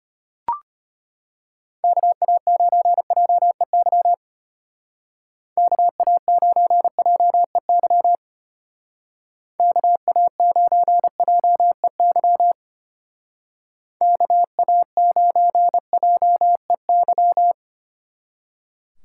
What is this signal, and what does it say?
Morse code, heard as a keyed tone, about 31 wpm, 700 Hz. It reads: KA9JEY KA9JEY KA9JEY KA9JEY
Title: Speed Racing - USA Call Signs 25wpm